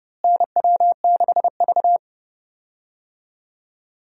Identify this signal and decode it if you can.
Morse code, heard as a keyed tone, about 31 wpm, 700 Hz. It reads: NW6V
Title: USA Call Signs - 4 Characters - Repeated 3x - 30wpm